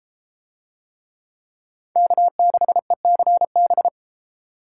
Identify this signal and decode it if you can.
Morse code, heard as a keyed tone, about 33 wpm, 700 Hz. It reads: K6ECB